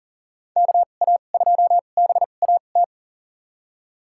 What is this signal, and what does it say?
Morse code, heard as a keyed tone, about 40 wpm, spaced as for 29 wpm, 700 Hz. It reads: KA2BAT
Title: USA Call Signs - 6 Characters - 2x Character Spacing - 40wpm